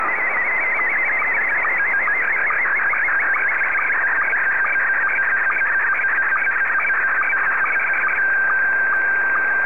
SSTV (slow-scan television), Scottie 1.
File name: sstv